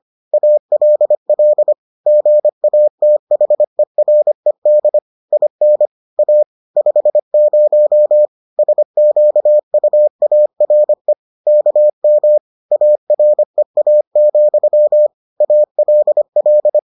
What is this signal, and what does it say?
Morse code, heard as a keyed tone, about 25 wpm, 600 Hz. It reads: ALL GATHERED IN A 50 SQUARE KM AREA, ALL